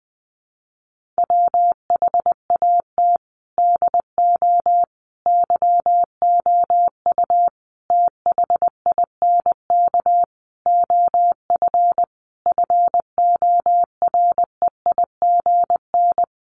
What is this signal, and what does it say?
Morse code, heard as a keyed tone, about 20 wpm, 700 Hz. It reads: WHAT DO YOU THINK OF FOREIGN
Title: Sentences from Top 1400 Words 20wpm